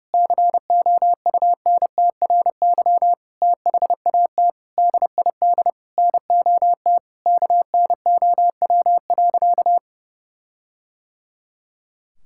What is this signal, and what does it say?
Morse code, heard as a keyed tone, about 30 wpm, 700 Hz. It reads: COUNTRY THAT DID NOT KNOW.
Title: Sentences from Top 1600 Words 30wpm